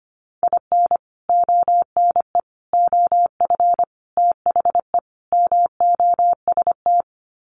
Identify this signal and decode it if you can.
Morse code, heard as a keyed tone, about 25 wpm, 700 Hz. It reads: IN ONE OF THE MOST
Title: Sets of 5 Words Encore 25wpm